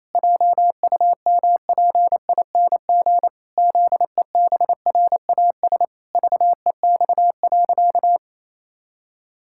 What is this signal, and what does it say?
Morse code, heard as a keyed tone, about 28 wpm, 700 Hz. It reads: JUMPING ZEBRAS VEX.
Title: Pangram Challenge v2 - 28wpm